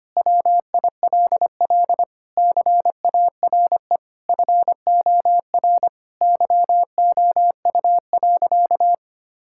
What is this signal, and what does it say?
Morse code, heard as a keyed tone, about 25 wpm, 700 Hz. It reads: WILL CARE FOR YOU.
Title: Sentences from Top 1000 Words - Review Part 6 - 25wpm